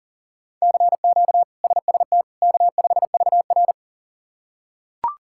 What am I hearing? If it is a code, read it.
Morse code, heard as a keyed tone, about 37 wpm, 700 Hz. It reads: CQ SST K5VR
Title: K1USN SST QSO Practice - 40wpm